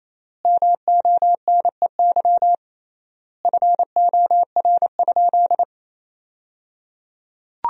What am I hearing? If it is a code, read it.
Morse code, heard as a keyed tone, about 28 wpm, 700 Hz. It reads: MONEY FOR?
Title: 3x Word-Spacing - Sentences from Top 400 Words - 28wpm